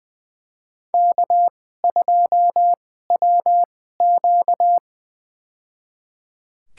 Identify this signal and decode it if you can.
Morse code, heard as a keyed tone, about 20 wpm, spaced as for 14 wpm, 700 Hz. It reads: K2WQ
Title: USA Call Signs - 2x Character Spacing - 20wpm